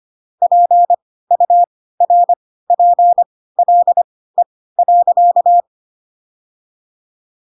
Morse code, heard as a keyed tone, about 25 wpm, spaced as for 16 wpm, 700 Hz. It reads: PURPLE.
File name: Pinocchio / Pinocchio ch1 10 25 ik1hge